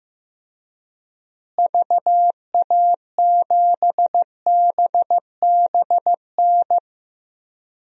Morse code, heard as a keyed tone, about 15 wpm, 700 Hz. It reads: VA7BBN
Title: DXCC-100 Call Signs - 6 Characters - Repeated 3x - 15wpm